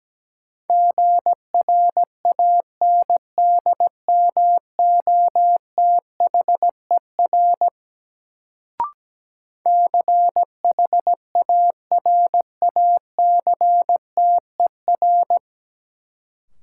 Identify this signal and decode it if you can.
Morse code, heard as a keyed tone, about 17 wpm, 700 Hz. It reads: GRANDMOTHER CHARACTER
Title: Sentences from Top 1200 Words - New Word Drill Alt - 17wpm